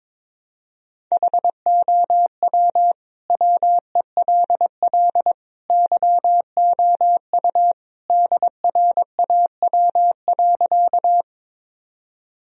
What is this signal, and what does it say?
Morse code, heard as a keyed tone, about 22 wpm, 700 Hz. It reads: HOW WELL YOU DRAW.
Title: Sentences from Top 1000 Words - Review Part 1 - 22wpm